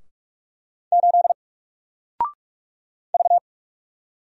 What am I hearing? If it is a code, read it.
Morse code, heard as a keyed tone, about 47 wpm, 700 Hz. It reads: 8 V
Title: Single Character (Letter:Number:Punctuation:Prosign) 45wpm